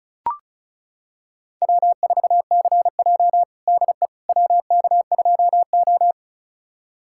Morse code, heard as a keyed tone, about 36 wpm, 700 Hz. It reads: W4CJ DE WK2O
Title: USA Callsign DE Callsign - 4 Characters - 35wpm